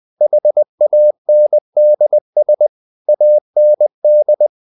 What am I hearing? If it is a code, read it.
Morse code, heard as a keyed tone, about 20 wpm, 600 Hz. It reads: HANDS AND
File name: the old man and the sea part 2 20wpm